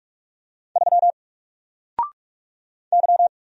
Morse code, heard as a keyed tone, about 44 wpm, 700 Hz. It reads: 3 Y
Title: Single Letter-Number Alt 45wpm